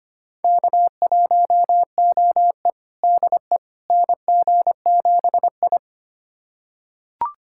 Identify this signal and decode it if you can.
Morse code, heard as a keyed tone, about 25 wpm, 700 Hz. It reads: K1OE DE NG7I
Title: USA Callsign DE Callsign - 4 Characters - 25wpm